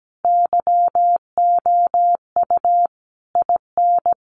Morse code, heard as a keyed tone, about 17 wpm, 700 Hz. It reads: YOU IN
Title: Sentences from Top 300 Words 17wpm